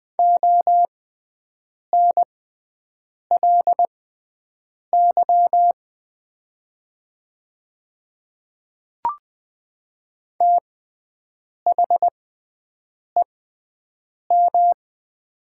Morse code, heard as a keyed tone, about 20 wpm, spaced as for 7 wpm, 700 Hz. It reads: ONLY THEM
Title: 6x Character Spacing - Top 200 Words - 20wpm